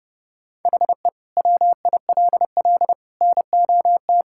Morse code, heard as a keyed tone, about 30 wpm, 700 Hz. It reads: HE WILL NOT
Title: Sentences from Top 1000 Words - Review Part 6 - 30wpm